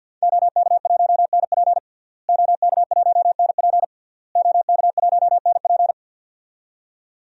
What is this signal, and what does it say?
Morse code, heard as a keyed tone, about 52 wpm, 700 Hz. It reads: OK1NP OK1NP OK1NP
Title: DXCC-100 Call Signs - 5 Characters - Repeated 3x - 50wpm